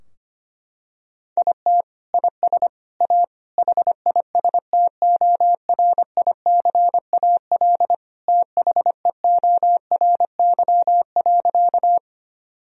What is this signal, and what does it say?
Morse code, heard as a keyed tone, about 25 wpm, 700 Hz. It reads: IT IS A HISTORICAL THEORY.
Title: Sentences from Top 1000 Words - Review Part 4 - 25wpm